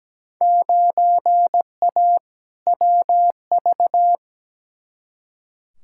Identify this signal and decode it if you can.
Morse code, heard as a keyed tone, about 17 wpm, 700 Hz. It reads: 9A WV
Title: Field Day Exchanges - 17wpm